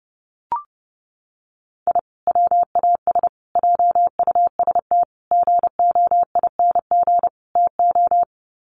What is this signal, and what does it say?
Morse code, heard as a keyed tone, about 30 wpm, 700 Hz. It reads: I WAS JUST GOING TO
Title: Sets of 5 Words Encore 30wpm